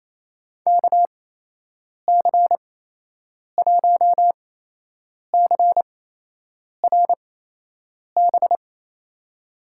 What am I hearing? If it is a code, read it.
Morse code, heard as a keyed tone, about 28 wpm, spaced as for 8 wpm, 700 Hz. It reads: KC1CRB
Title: USA Call Signs - 6 Characters - 8x Character Spacing - 28wpm